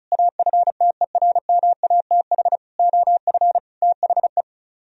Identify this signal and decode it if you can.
Morse code, heard as a keyed tone, about 43 wpm, 700 Hz. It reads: AFTERMATH OF THE
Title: Speed-Racing - Sets of 5 Words 35wpm